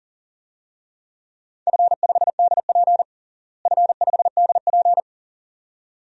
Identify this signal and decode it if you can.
Morse code, heard as a keyed tone, about 40 wpm, 700 Hz. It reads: F5DP F5DP
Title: DXCC-100 Call Signs - 4 Characters - Repeated 2x - 40wpm